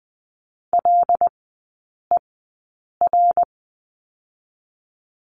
Morse code, heard as a keyed tone, about 20 wpm, spaced as for 8 wpm, 700 Hz. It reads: LER
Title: 107 Lesson 18e - Letter m - Three Characters 20wpm